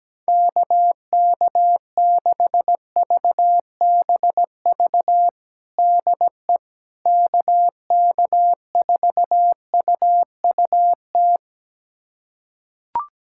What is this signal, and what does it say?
Morse code, heard as a keyed tone, about 17 wpm, 700 Hz. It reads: KK6VBV DE KK4UUT
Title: USA Callsign DE Callsign - 6 Characters - 17wpm